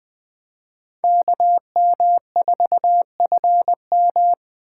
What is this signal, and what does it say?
Morse code, heard as a keyed tone, about 24 wpm, 700 Hz. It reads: KM4FM
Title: Speed Racing - USA Call Signs 20wpm